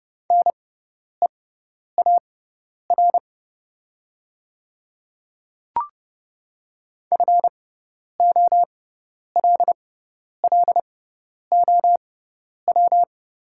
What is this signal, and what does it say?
Morse code, heard as a keyed tone, about 31 wpm, spaced as for 10 wpm, 700 Hz. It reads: NEAR FOLLOW
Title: 6x Character Spacing - Top 200 Words - 30wpm